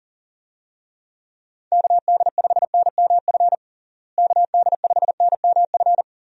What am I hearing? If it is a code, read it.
Morse code, heard as a keyed tone, about 37 wpm, 700 Hz. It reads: KD5NMF KD5NMF
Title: USA Call Signs - 6 Characters - Repeated 3x - 40wpm